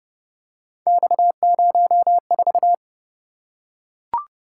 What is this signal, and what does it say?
Morse code, heard as a keyed tone, about 29 wpm, 700 Hz. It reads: X04